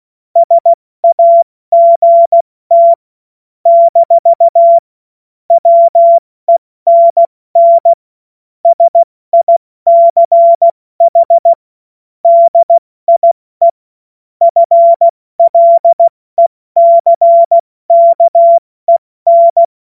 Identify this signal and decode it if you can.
Morse code, heard as a keyed tone, about 16 wpm, 680 Hz. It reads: SAGT - WENN SICH DIE FLECKEN
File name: Morsekurs / Witze / Witz 06. Januar 2026 70 F80